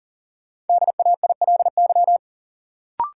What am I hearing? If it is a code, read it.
Morse code, heard as a keyed tone, about 40 wpm, 700 Hz. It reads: DAILY
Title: Sentences from Top 1100 Words - New Word Drill Alt - 40wpm